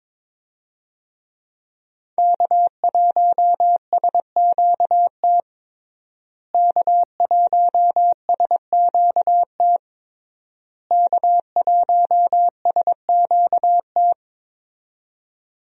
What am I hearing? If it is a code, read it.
Morse code, heard as a keyed tone, about 22 wpm, 700 Hz. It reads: K1SQT K1SQT K1SQT
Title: USA Call Signs - 5 Characters - Repeated 3x - 22wpm